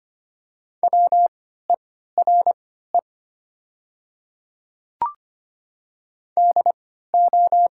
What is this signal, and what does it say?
Morse code, heard as a keyed tone, about 25 wpm, spaced as for 14 wpm, 700 Hz. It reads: WERE DO
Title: 3x Character Spacing - Top 300 Words - 25wpm